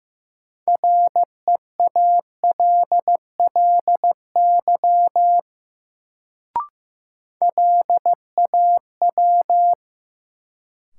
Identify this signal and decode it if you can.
Morse code, heard as a keyed tone, about 15 wpm, 700 Hz. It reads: REALLY LAW
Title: Sentences from Top 600 Words - New Word Drill - 15wpm